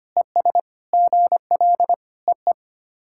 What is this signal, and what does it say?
Morse code, heard as a keyed tone, about 25 wpm, 700 Hz. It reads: ES GL EE
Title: SOTA QSO Practice 25wpm